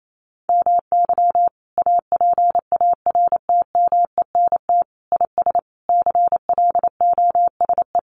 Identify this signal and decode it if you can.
Morse code, heard as a keyed tone, about 28 wpm, 700 Hz. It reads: MY APARTMENT IS CLOSE
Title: Sentences from Top 900 Words 28wpm